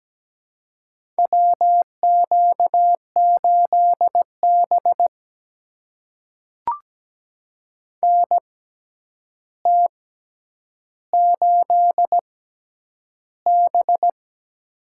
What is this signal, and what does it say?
Morse code, heard as a keyed tone, about 17 wpm, spaced as for 9 wpm, 700 Hz. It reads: WQ8B NT8B
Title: USA Call Signs - 4 Characters - 6x Character Spacing - 17wpm